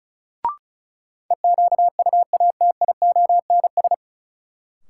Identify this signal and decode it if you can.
Morse code, heard as a keyed tone, about 36 wpm, 700 Hz. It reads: EQUATIONS
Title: Sets of 5 Words - New Word Drill - 35wpm